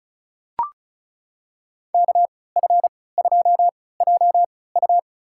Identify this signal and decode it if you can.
Morse code, heard as a keyed tone, about 36 wpm, spaced as for 20 wpm, 700 Hz. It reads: KF2JU
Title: USA Call Signs - 3x Character Spacing - 35wpm